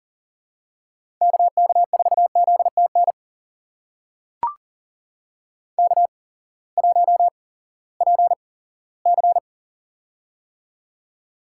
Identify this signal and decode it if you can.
Morse code, heard as a keyed tone, about 40 wpm, spaced as for 20 wpm, 700 Hz. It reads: KK4ZTN K1PC
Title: USA Call Signs - 8x Character Spacing - 40wpm